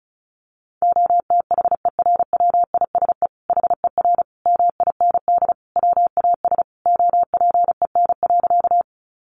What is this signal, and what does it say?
Morse code, heard as a keyed tone, about 34 wpm, 700 Hz. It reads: OTHERWISE HER MIND WAS OPEN.
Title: Sentences from Top 1200 Words 35wpm